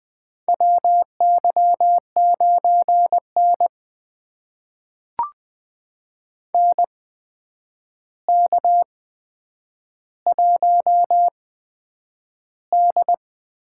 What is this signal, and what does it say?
Morse code, heard as a keyed tone, about 20 wpm, spaced as for 9 wpm, 700 Hz. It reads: WY9N NK1D
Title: USA Call Signs - 4 Characters - 8x Character Spacing - 20wpm